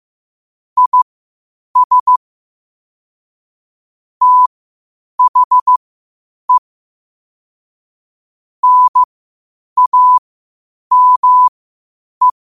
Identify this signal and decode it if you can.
Morse code, heard as a keyed tone, about 15 wpm, spaced as for 8 wpm, 1000 Hz. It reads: IS THE NAME
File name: qotdfiles / 8wpm